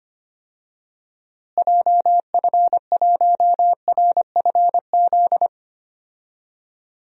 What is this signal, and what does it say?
Morse code, heard as a keyed tone, about 25 wpm, 700 Hz. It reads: JF1RFZ
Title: DXCC-100 Call Signs - 6 Characters - Repeated 2x - 25wpm